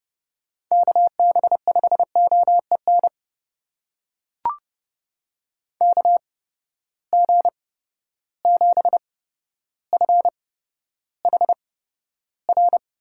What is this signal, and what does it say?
Morse code, heard as a keyed tone, about 30 wpm, spaced as for 13 wpm, 700 Hz. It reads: KB5OEN KG7FHR
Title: USA Call Signs - 8x Character Spacing - 30wpm